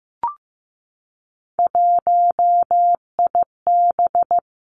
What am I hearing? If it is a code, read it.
Morse code, heard as a keyed tone, about 15 wpm, 700 Hz. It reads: 1IB
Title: Three Challenging Letter-Number Combinations 15wpm